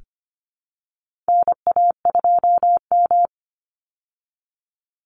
Morse code, heard as a keyed tone, about 25 wpm, 700 Hz. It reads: NA2M
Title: Call Signs Repeated 2x - 25wpm